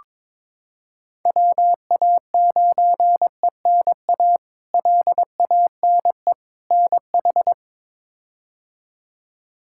Morse code, heard as a keyed tone, about 22 wpm, 700 Hz. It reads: WA9ENA LANE NH
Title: K1USN SST QSO Practice - 22wpm